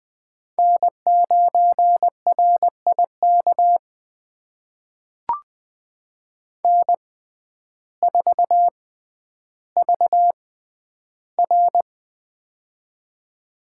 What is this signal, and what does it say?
Morse code, heard as a keyed tone, about 20 wpm, spaced as for 11 wpm, 700 Hz. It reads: N9RIK N4VR
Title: USA Call Signs - 6x Character Spacing - 20wpm